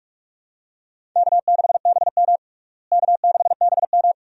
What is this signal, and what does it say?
Morse code, heard as a keyed tone, about 46 wpm, 700 Hz. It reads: K6BM K6BM
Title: USA Call Signs - 4 Characters - Repeated 2x - 45wpm